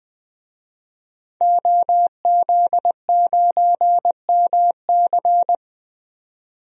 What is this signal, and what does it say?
Morse code, heard as a keyed tone, about 20 wpm, 700 Hz. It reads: OZ9MC
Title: DXCC-100 Call Signs - 5 Characters - 20wpm